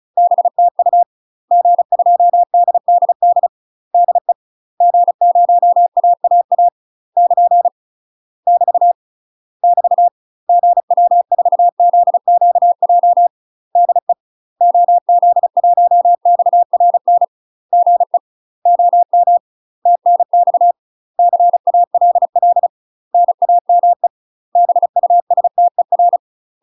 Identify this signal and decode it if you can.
Morse code, heard as a keyed tone, about 35 wpm, 700 Hz. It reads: BTU G2DDD DE G0AAA <KN> <BT> <BT> GW4ZQJ DE OZ1XRN GE OM TNX CALL NAME BUSTER
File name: UK MASTER QSOs TRAINING / 35WPM / UK MASTER QSOs 0000